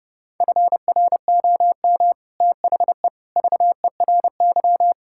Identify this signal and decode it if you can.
Morse code, heard as a keyed tone, about 30 wpm, 700 Hz. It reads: FROM THE VERY
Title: Sentences from Top 1600 Words 30wpm